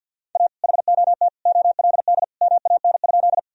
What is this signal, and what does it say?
Morse code, heard as a keyed tone, about 50 wpm, 700 Hz. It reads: A HOT OLD MAN?
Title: Sentences from Top 1000 Words - Review Part 3 - 50wpm